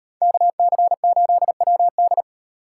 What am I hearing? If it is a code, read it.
Morse code, heard as a keyed tone, about 37 wpm, 700 Hz. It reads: KC8WD